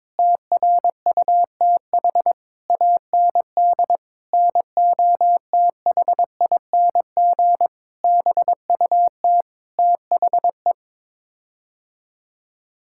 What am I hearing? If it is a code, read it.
Morse code, heard as a keyed tone, about 22 wpm, 700 Hz. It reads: TRUTH AND NOTHING BUT THE
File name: Sets of 9 Words - 22wpm